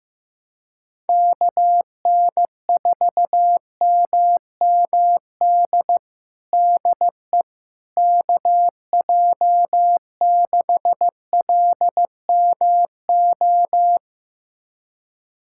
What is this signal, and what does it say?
Morse code, heard as a keyed tone, about 15 wpm, 700 Hz. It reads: KN4MMD DE KJ6LMO
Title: USA Callsign DE Callsign - 6 Characters - 15wpm